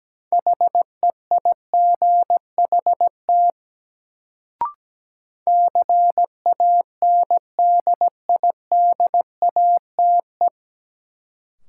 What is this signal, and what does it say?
Morse code, heard as a keyed tone, about 17 wpm, 700 Hz. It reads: HEIGHT CANDIDATE
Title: Sentences from Top 1600 Words - New Word Drill Alt - 17wpm